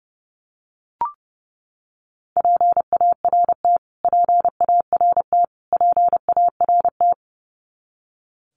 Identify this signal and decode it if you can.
Morse code, heard as a keyed tone, about 30 wpm, 700 Hz. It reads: PART PART PART
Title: Top 200 Words Repeated 30wpm